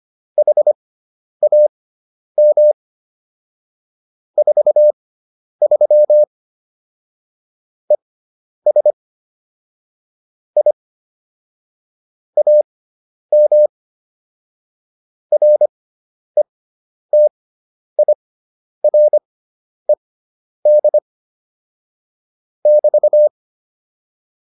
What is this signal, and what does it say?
Morse code, heard as a keyed tone, about 25 wpm, spaced as for 10 wpm, 600 Hz. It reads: HAM 43 ES I AM RETIRED <BT>